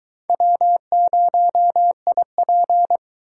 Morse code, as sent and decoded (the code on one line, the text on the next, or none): .-- ----- .. .--.
W0IP